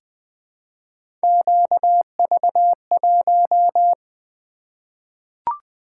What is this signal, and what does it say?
Morse code, heard as a keyed tone, about 20 wpm, 700 Hz. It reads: QV1
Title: Three Letter-Number Combinations - 20wpm